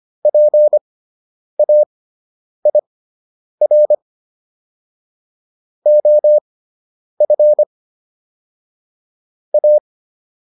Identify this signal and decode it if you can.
Morse code, heard as a keyed tone, about 25 wpm, spaced as for 9 wpm, 600 Hz. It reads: PAIR OF A